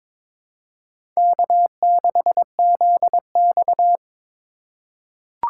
Morse code, heard as a keyed tone, about 22 wpm, 700 Hz. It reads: K6ZX
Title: K1USN SST QSO Practice - 22wpm